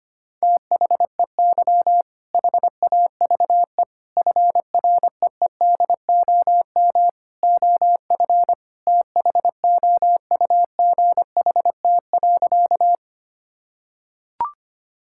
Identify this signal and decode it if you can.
Morse code, heard as a keyed tone, about 25 wpm, 700 Hz. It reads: THEY HAVE FREEDOM OF THOUGHT.
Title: Sentences from Top 1000 Words - Review Part 5 - 25wpm